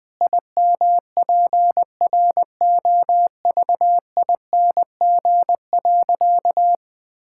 Morse code, heard as a keyed tone, about 20 wpm, 700 Hz. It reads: IMPROVING.
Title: Sentences with 8-Word Phrases - 20wpm